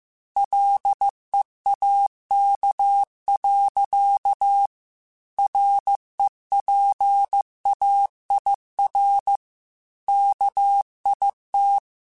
Morse code, heard as a keyed tone, about 15 wpm, 800 Hz. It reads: LEAK. REPAIR KIT
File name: Chuckles / Chuckles-mp3s / Guaranteed @13